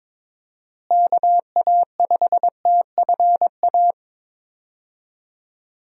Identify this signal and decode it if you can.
Morse code, heard as a keyed tone, about 22 wpm, 700 Hz. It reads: KA5TFA
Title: USA Call Signs - 6 Characters - Repeated 2x - 22wpm